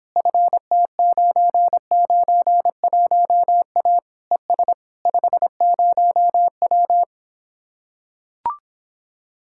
Morse code, heard as a keyed tone, about 26 wpm, 700 Hz. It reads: FT991A ES 50W